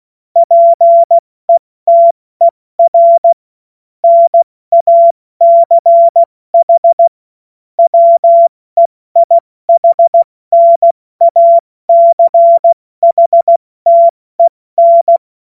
Morse code, heard as a keyed tone, about 16 wpm, 680 Hz. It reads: PETER NACH WEIHNACHTEN
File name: Morsekurs / Witze / Witz 28. März 2023 70 F80